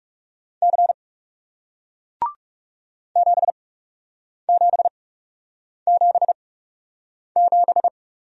Morse code, heard as a keyed tone, about 40 wpm, spaced as for 8 wpm, 700 Hz. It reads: C 7777